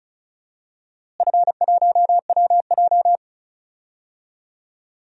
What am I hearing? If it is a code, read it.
Morse code, heard as a keyed tone, about 36 wpm, 690 Hz. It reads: F1WJ
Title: DXCC-100 Call Signs - 4 Characters - Repeated 3x - 35wpm